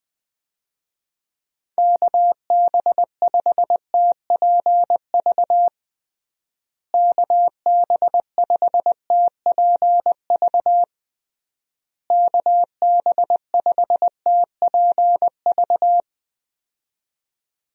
Morse code, heard as a keyed tone, about 20 wpm, 700 Hz. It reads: KB5TPV KB5TPV KB5TPV
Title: USA Call Signs - 6 Characters - Repeated 3x - 20wpm